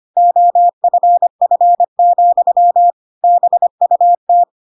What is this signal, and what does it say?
Morse code, heard as a keyed tone, about 25 wpm, 700 Hz. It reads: OFF, BUT
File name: Pinocchio / Pinocchio ch5 25 25 ik1hge